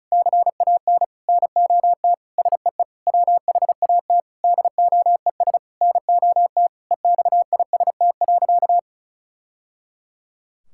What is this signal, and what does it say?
Morse code, heard as a keyed tone, about 36 wpm, 700 Hz. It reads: CAN NOT SEE WHAT DOES NOT EXIST.
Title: Sentences from Top 1000 Words - Review Part 5 - 35wpm